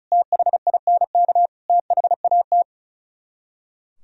Morse code, heard as a keyed tone, about 44 wpm, 700 Hz. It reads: THINK THAT